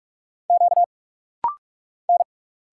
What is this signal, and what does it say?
Morse code, heard as a keyed tone, about 46 wpm, 700 Hz. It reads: Q N
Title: Single Letters - Rapid-Fire 45wpm